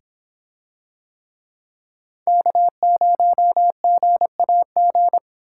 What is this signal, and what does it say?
Morse code, heard as a keyed tone, about 26 wpm, 700 Hz. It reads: K0GAG